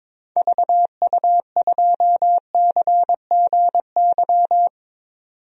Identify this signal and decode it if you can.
Morse code, heard as a keyed tone, about 21 wpm, 700 Hz. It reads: VU2CGY
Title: DXCC-100 Call Signs - 6 Characters - 22wpm